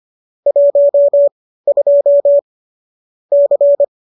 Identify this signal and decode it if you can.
Morse code, heard as a keyed tone, about 25 wpm, spaced as for 15 wpm, 570 Hz. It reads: 12 C
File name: qso204 15